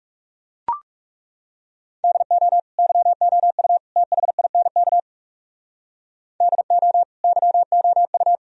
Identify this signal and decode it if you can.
Morse code, heard as a keyed tone, about 37 wpm, 700 Hz. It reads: DO YOU THINK DO YOU
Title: Speed-Racing - Sets of 3 Words 30wpm